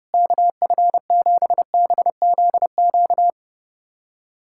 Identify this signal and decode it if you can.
Morse code, heard as a keyed tone, about 30 wpm, 700 Hz. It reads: KF7BZQ
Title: USA Callsign DE Callsign - 6 Characters - 30wpm